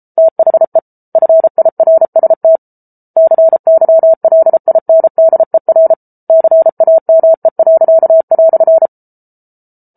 Morse code, heard as a keyed tone, about 34 wpm, 650 Hz. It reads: THE FIRST CYLINDER CAME."